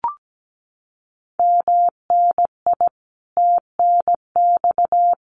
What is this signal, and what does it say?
Morse code, heard as a keyed tone, about 17 wpm, 700 Hz. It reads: MNI TNX
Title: Common QSO Phrases 17wpm